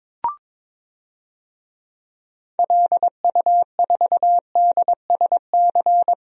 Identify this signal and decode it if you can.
Morse code, heard as a keyed tone, about 22 wpm, 700 Hz. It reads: LU4DSC
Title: DXCC-100 Call Signs - 6 Characters - 22wpm